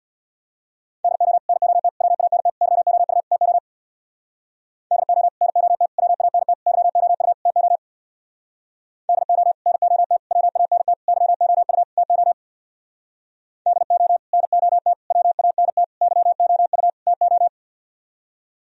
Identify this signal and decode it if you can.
Morse code, heard as a keyed tone, about 64 wpm, 700 Hz. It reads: DO NOT WANT YOU TO DO NOT WANT YOU TO DO NOT WANT YOU TO DO NOT WANT YOU TO